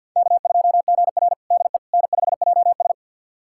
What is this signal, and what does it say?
Morse code, heard as a keyed tone, about 49 wpm, 700 Hz. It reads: K2GR DE N5JS
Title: USA Callsign DE Callsign - 4 Characters - 50wpm